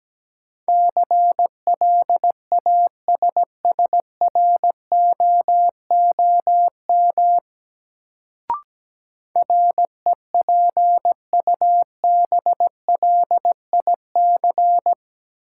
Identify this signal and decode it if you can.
Morse code, heard as a keyed tone, about 17 wpm, 700 Hz. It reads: CLASSROOM REPUBLIC
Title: Sentences from Top 1500 Words - New Word Drill Alt - 17wpm